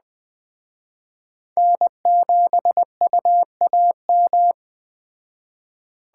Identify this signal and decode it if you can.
Morse code, heard as a keyed tone, about 21 wpm, 700 Hz. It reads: N7UAM
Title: Speed Racing - USA 1x3 Call Signs 17wpm